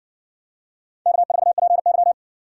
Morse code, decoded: K4PY

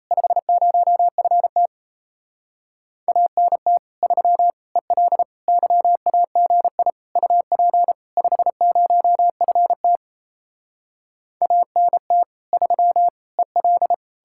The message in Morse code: 50FT ANT 3 EL YAGI UP 50FT ANT 3 EL

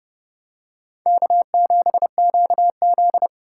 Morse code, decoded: K7QZ

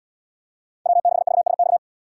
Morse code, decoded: W6VSX

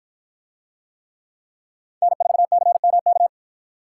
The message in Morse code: N4CMK

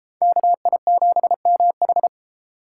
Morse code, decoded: KI7MH

